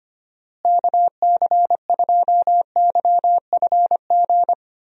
Morse code, heard as a keyed tone, about 25 wpm, 700 Hz. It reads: KC2YFG
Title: USA Call Signs - 6 Characters - Repeated 2x - 25wpm